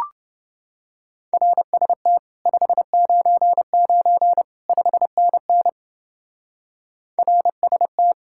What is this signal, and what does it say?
Morse code, heard as a keyed tone, about 28 wpm, 700 Hz. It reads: RST 599 5NN RST